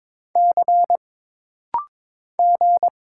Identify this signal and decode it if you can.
Morse code, heard as a keyed tone, about 23 wpm, 700 Hz. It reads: C G
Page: This rapid-fire timing is geared to push your ICR (Instant Character Recognition) skills.